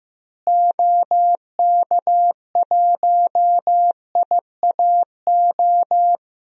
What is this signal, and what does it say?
Morse code, heard as a keyed tone, about 15 wpm, 700 Hz. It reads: OK1IAO
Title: DXCC-100 Call Signs - 6 Characters - Repeated 3x - 15wpm